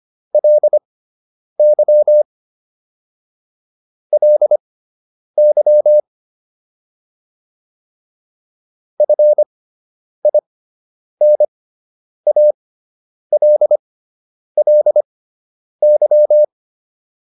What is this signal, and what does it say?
Morse code, heard as a keyed tone, about 25 wpm, spaced as for 9 wpm, 600 Hz. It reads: LY LY FINALLY
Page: fundamental-practice-files